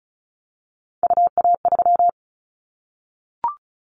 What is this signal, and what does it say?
Morse code, heard as a keyed tone, about 36 wpm, 700 Hz. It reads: UA3
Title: Callsign Prefixes for Top 25 DXCC - 35wpm